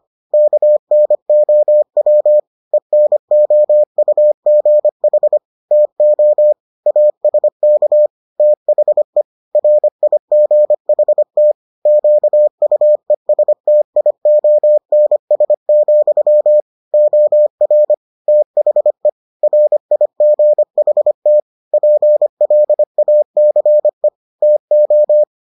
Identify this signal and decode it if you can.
Morse code, heard as a keyed tone, about 25 wpm, 600 Hz. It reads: KNOW ENOUGH TO ASK THE RIGHT QUESTIONS, OR THE RIGHT PLACE TO